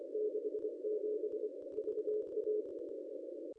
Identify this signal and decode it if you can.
Morse code, heard as a keyed tone, about 24 wpm, 440 Hz. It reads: 67 VA
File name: contests / ss-cw-k5tr-op-n5rz -1600